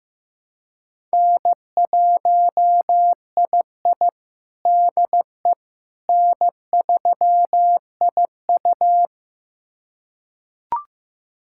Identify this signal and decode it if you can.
Morse code, heard as a keyed tone, about 15 wpm, 700 Hz. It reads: N1II DE N3IU
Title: USA Callsign DE Callsign - 4 Characters - 15wpm